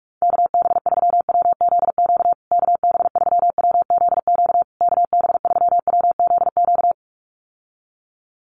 Morse code, heard as a keyed tone, about 44 wpm, 700 Hz. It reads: KB3WZQ KB3WZQ KB3WZQ
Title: USA Call Signs Repeated 3x 45wpm